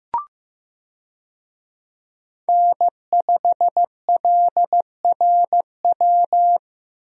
Morse code, heard as a keyed tone, about 15 wpm, 700 Hz. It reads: N5LRW
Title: USA Call Signs - 5 Characters - Repeated 2x - 15wpm